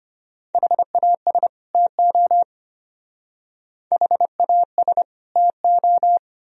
Morse code, heard as a keyed tone, about 32 wpm, 700 Hz. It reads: HAS TO HAS TO